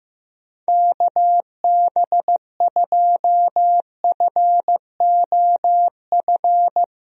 Morse code, heard as a keyed tone, about 15 wpm, 700 Hz. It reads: KB2FOF
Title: USA Call Signs - 6 Characters - Repeated 2x - 15wpm